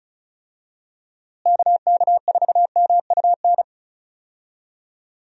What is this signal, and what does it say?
Morse code, heard as a keyed tone, about 34 wpm, 700 Hz. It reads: KK4MUN
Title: USA Call Signs - 6 Characters - Repeated 2x - 35wpm